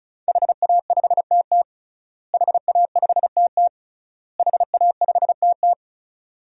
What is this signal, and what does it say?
Morse code, heard as a keyed tone, about 36 wpm, 700 Hz. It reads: HA5TT HA5TT HA5TT
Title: DXCC-100 Call Signs - 5 Characters - Repeated 3x - 35wpm